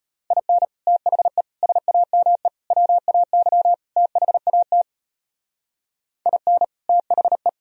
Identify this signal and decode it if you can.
Morse code, heard as a keyed tone, about 31 wpm, 700 Hz. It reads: IN THE SAME WAY THAT IN THE